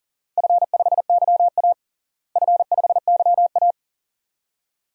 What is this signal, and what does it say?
Morse code, heard as a keyed tone, about 40 wpm, 700 Hz. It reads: F5YA F5YA
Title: DXCC-100 Call Signs - 4 Characters - Repeated 2x - 40wpm